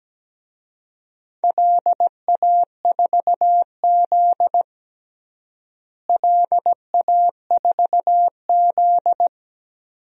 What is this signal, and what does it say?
Morse code, heard as a keyed tone, about 17 wpm, 700 Hz. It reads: LA4Z LA4Z
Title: DXCC-100 Call Signs - 4 Characters - Repeated 2x - 17wpm